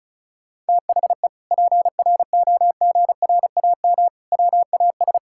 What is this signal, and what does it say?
Morse code, heard as a keyed tone, about 34 wpm, 700 Hz. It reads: THE PROGRAM WAS
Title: Sentences from Top 1000 Words - Review Part 5 - 35wpm